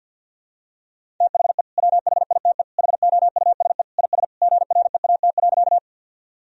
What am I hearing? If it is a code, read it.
Morse code, heard as a keyed tone, about 47 wpm, 700 Hz. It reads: THE WHITE HOUSE IS GREAT.
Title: Sentences from Top 1000 Words - Review Part 1 - 50wpm